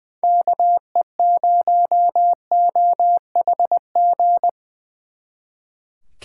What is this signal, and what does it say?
Morse code, heard as a keyed tone, about 24 wpm, 700 Hz. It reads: KE0OHG
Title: Speed Racing - USA 2x3 Call Signs 20wpm